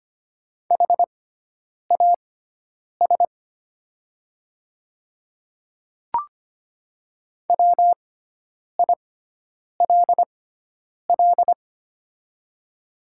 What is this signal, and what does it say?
Morse code, heard as a keyed tone, about 25 wpm, spaced as for 9 wpm, 700 Hz. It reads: HAS WILL
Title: 6x Character Spacing - Top 100 Words - 25wpm